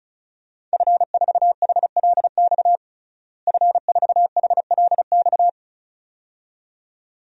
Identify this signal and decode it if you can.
Morse code, heard as a keyed tone, about 34 wpm, 700 Hz. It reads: F4HLX F4HLX
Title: DXCC-100 Call Signs - 5 Characters - Repeated 2x - 35wpm